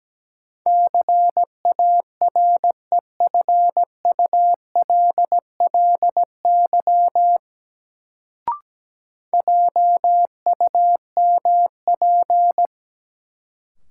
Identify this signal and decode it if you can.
Morse code, heard as a keyed tone, about 17 wpm, 700 Hz. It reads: CAREFULLY JUMP
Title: Sentences from Top 1300 Words - New Word Drill - 17wpm